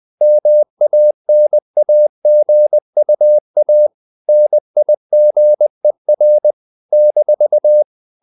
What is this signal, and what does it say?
Morse code, heard as a keyed tone, about 20 wpm, 600 Hz. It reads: MANAGUA NIGER -
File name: CW / Practice / NationalCapitals20